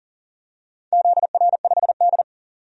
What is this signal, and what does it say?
Morse code, heard as a keyed tone, about 40 wpm, 700 Hz. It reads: ZR5D
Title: DXCC-100 Call Signs - 4 Characters - 40wpm